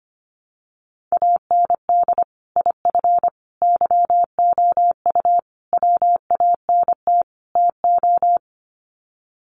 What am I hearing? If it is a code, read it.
Morse code, heard as a keyed tone, about 25 wpm, 700 Hz. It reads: AND IF YOU WANT TO
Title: Sets of 5 Words Encore 25wpm